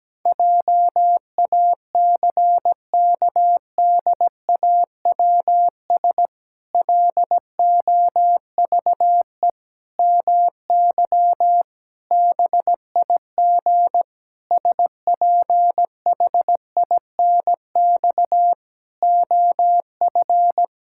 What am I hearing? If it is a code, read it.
Morse code, heard as a keyed tone, about 17 wpm, 700 Hz. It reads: JACKDAWS LOVE MY BIG SPHINX OF
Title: Pangram Challenge v2 - 17wpm